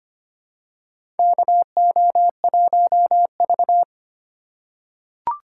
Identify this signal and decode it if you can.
Morse code, heard as a keyed tone, about 25 wpm, 700 Hz. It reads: KO1V
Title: USA Call Signs - 4 Characters - 3x Character Spacing - 25wpm